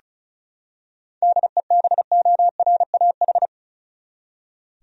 Morse code, heard as a keyed tone, about 34 wpm, 700 Hz. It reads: DEBORAH
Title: USA Common Names - 35wpm